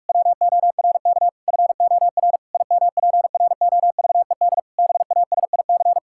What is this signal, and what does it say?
Morse code, heard as a keyed tone, about 46 wpm, 700 Hz. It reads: WORK FOR IMPROVED BASIC
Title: Pangram Challenge v2 - 45wpm